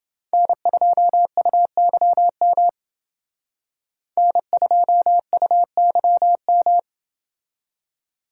Morse code, heard as a keyed tone, about 28 wpm, 700 Hz. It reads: N2UYM N2UYM